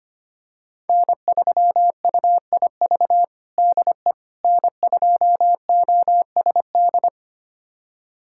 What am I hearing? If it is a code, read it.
Morse code, heard as a keyed tone, about 25 wpm, 700 Hz. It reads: N3UIV DE N2OSD
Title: USA Callsign DE Callsign - 5 Characters - 25wpm